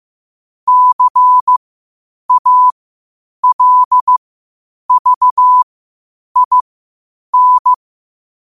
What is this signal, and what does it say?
Morse code, heard as a keyed tone, about 15 wpm, spaced as for 8 wpm, 1000 Hz. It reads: CALVIN
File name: qotdfiles / 8wpm